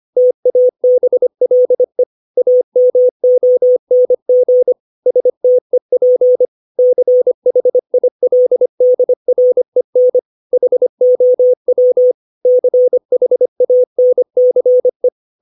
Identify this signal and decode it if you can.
Morse code, heard as a keyed tone, about 25 wpm, 500 Hz. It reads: TABLE AMONG STEP CHILDREN HOW CHANCE
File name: 632 words / 632 Words 25WPM